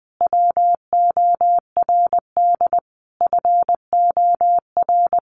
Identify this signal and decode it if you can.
Morse code, heard as a keyed tone, about 20 wpm, 700 Hz. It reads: WORD FOR
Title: Sentences from Top 1300 Words 20wpm